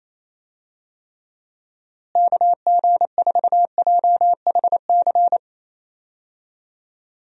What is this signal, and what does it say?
Morse code, heard as a keyed tone, about 28 wpm, 700 Hz. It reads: KG4JHC
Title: USA Call Signs - 6 Characters - Repeated 2x - 28wpm